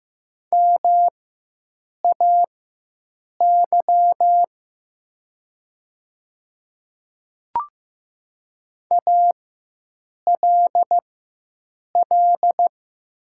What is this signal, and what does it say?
Morse code, heard as a keyed tone, about 15 wpm, spaced as for 7 wpm, 700 Hz. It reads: MAY ALL
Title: 4x Character Spacing - Top 100 Words - 15wpm